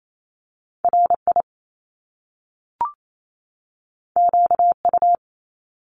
Morse code, heard as a keyed tone, about 28 wpm, 700 Hz. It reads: RI QU
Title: Top Two Letter Combinations in Words 28wpm